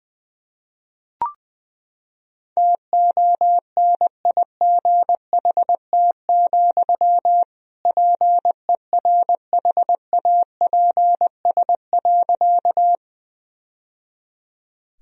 Morse code, heard as a keyed tone, about 20 wpm, 700 Hz. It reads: TONIGHT, PERHAPS.
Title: Sentences from Top 1000 Words - Review Part 3 - 20wpm